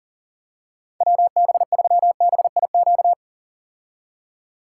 Morse code, heard as a keyed tone, about 40 wpm, 700 Hz. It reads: WB3BIQ
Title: K1USN SST QSO Practice - 40wpm